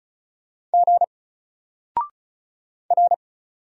Morse code, heard as a keyed tone, about 35 wpm, 700 Hz. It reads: G R
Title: Single Character (Letter:Number:Punctuation:Prosign) 35wpm